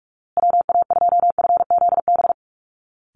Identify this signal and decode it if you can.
Morse code, heard as a keyed tone, about 44 wpm, 700 Hz. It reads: WA2FZB
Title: Call Signs Repeated 2x - 45wpm